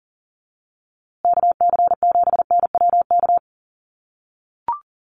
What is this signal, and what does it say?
Morse code, heard as a keyed tone, about 40 wpm, 700 Hz. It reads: KC7NWK
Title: USA Call Signs Repeated 3x 40wpm